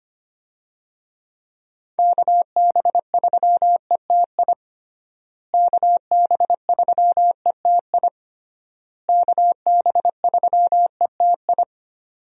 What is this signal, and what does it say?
Morse code, heard as a keyed tone, about 25 wpm, 700 Hz. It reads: KB3ETI KB3ETI KB3ETI
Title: USA Call Signs - 6 Characters - Repeated 3x - 25wpm